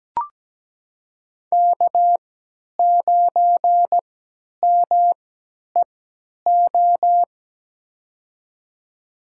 Morse code, heard as a keyed tone, about 17 wpm, spaced as for 10 wpm, 700 Hz. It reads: K9MEO
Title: USA Call Signs - 5 Characters - 3x Character Spacing - 17wpm